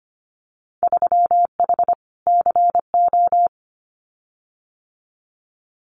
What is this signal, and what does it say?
Morse code, heard as a keyed tone, about 25 wpm, 700 Hz. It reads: 3H CO
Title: Winter Field Day Exchanges 25wpm